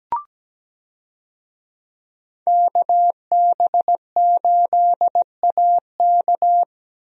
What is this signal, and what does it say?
Morse code, heard as a keyed tone, about 17 wpm, 700 Hz. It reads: KB8AK
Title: USA Call Signs - 5 Characters - Repeated 3x - 17wpm